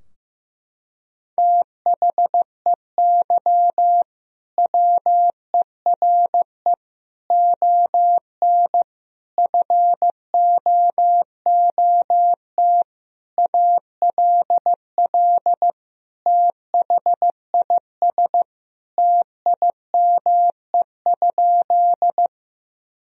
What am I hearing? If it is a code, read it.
Morse code, heard as a keyed tone, about 15 wpm, 700 Hz. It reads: THEY WERE ON FOOT ALL THIS TIME?
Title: Sentences from Top 1000 Words - Review Part 3 - 15wpm